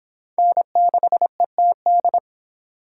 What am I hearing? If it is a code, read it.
Morse code, heard as a keyed tone, about 23 wpm, 700 Hz. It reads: N6ETD